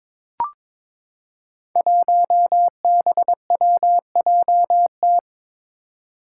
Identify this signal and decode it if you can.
Morse code, heard as a keyed tone, about 22 wpm, 700 Hz. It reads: 1BWJT
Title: Five Challenging Letter-Number Combinations 22wpm